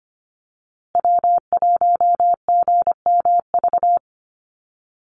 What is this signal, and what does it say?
Morse code, heard as a keyed tone, about 25 wpm, 700 Hz. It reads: W1GMV
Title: USA Call Signs Repeated 3x 25wpm